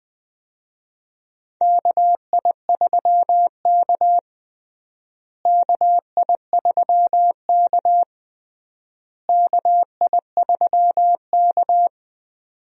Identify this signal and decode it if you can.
Morse code, heard as a keyed tone, about 20 wpm, 700 Hz. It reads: KI3K KI3K KI3K
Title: USA Call Signs - 4 Characters - Repeated 3x - 20wpm